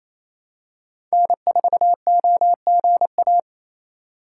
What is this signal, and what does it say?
Morse code, heard as a keyed tone, about 28 wpm, 700 Hz. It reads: N4OGA
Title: USA Call Signs - 5 Characters - 2x Character Spacing - 28wpm